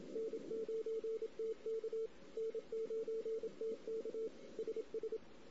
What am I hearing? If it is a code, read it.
Morse code, heard as a keyed tone, about 26 wpm, 440 Hz. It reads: N9TK N9TK SS
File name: CW / 2008-ss-cw-k5tr-op-n5rz -16:00